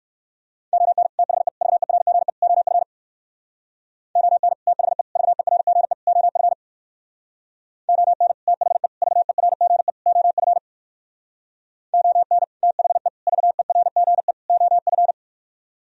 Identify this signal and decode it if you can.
Morse code, heard as a keyed tone, about 66 wpm, 700 Hz. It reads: ON THE VERGE OF ON THE VERGE OF ON THE VERGE OF ON THE VERGE OF